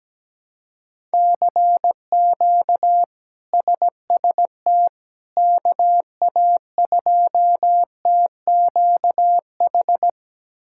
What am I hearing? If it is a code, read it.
Morse code, heard as a keyed tone, about 17 wpm, 700 Hz. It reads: CQ SST KA2TQH
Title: K1USN SST QSO Practice - 17wpm